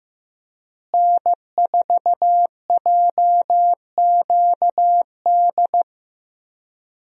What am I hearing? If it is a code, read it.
Morse code, heard as a keyed tone, about 15 wpm, 700 Hz. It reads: N4JQD
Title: USA Call Signs - 5 Characters - Repeated 3x - 15wpm